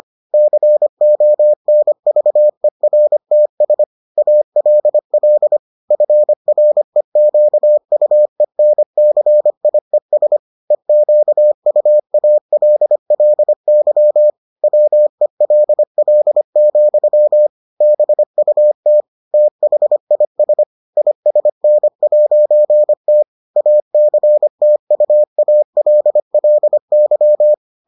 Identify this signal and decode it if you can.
Morse code, heard as a keyed tone, about 25 wpm, 600 Hz. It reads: CONVERTS ALL FREQUENCIES EQUALLY WELL, BUT THIS ISN'T ACTUALLY